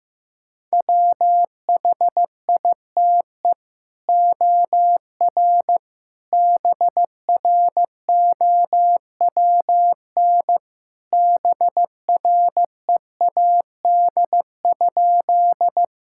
Morse code, heard as a keyed tone, about 15 wpm, 700 Hz. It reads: WHITE OR BROWN BREAD?
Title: Sentences from Top 1600 Words 15wpm